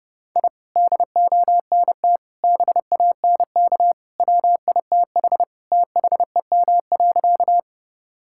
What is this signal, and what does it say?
Morse code, heard as a keyed tone, about 30 wpm, 700 Hz. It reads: I DONT BANK WITH THEM.
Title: Sentences from Top 1000 Words - Review Part 6 - 30wpm